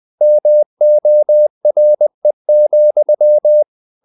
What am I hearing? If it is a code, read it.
Morse code, heard as a keyed tone, about 20 wpm, 600 Hz. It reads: MORE,
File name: the old man and the sea / the old man and the sea part 1 20wpm